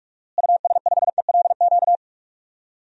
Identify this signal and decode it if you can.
Morse code, heard as a keyed tone, about 46 wpm, 700 Hz. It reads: US5ELQ
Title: DXCC-100 Call Signs - 6 Characters - Repeated 2x - 45wpm